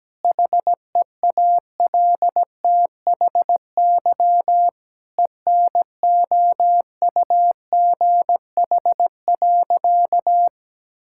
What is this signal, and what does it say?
Morse code, heard as a keyed tone, about 17 wpm, 700 Hz. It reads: HEALTHY ENOUGH.
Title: Sentences from Top 1000 Words - Review Part 5 - 17wpm